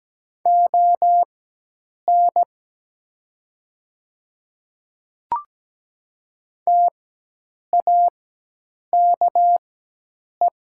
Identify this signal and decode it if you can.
Morse code, heard as a keyed tone, about 17 wpm, spaced as for 8 wpm, 700 Hz. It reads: ON TAKE
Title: 4x Character Spacing - Top 200 Words - 17wpm